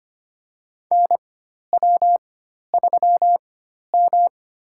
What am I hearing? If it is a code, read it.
Morse code, heard as a keyed tone, about 25 wpm, spaced as for 12 wpm, 700 Hz. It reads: NW3M
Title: USA Call Signs - 4x Character Spacing - 25wpm